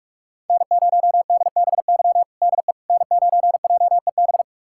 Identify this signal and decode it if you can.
Morse code, heard as a keyed tone, about 46 wpm, 700 Hz. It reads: N0DBY DE N9JEB
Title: USA Callsign DE Callsign 45wpm